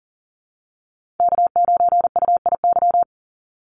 Morse code, heard as a keyed tone, about 40 wpm, 700 Hz. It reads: K9UIY
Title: Call Signs Repeated 3x - 40wpm